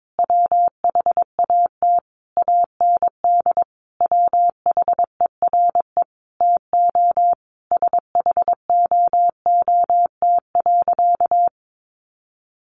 Morse code, heard as a keyed tone, about 22 wpm, 700 Hz. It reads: WHAT AND WHERE TO SHOOT.
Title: A Sound Of Thunder 22wpm